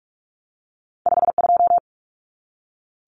Morse code, heard as a keyed tone, about 46 wpm, 700 Hz. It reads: 52
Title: Common Cut Numbers - 2-Digit 45wpm